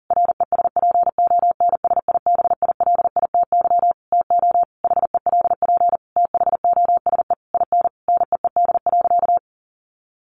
Morse code, heard as a keyed tone, about 40 wpm, 700 Hz. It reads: RESPONSIBILITY TO HELP THOSE IN NEED.
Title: Sentences from Top 1200 Words 40wpm